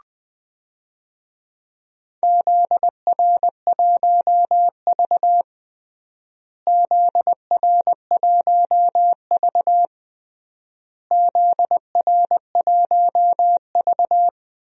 Morse code, heard as a keyed tone, about 20 wpm, 700 Hz. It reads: ZR1V ZR1V ZR1V
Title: DXCC-100 Call Signs - 4 Characters - Repeated 3x - 20wpm